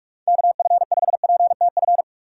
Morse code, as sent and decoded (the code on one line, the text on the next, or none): -.- ..-. ..... .--. - ..-.
KF5PTF